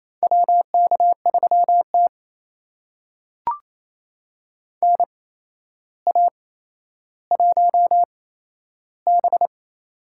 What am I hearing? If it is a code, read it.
Morse code, heard as a keyed tone, about 28 wpm, spaced as for 12 wpm, 700 Hz. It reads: WK3T NA1B
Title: USA Call Signs - 4 Characters - 8x Character Spacing - 28wpm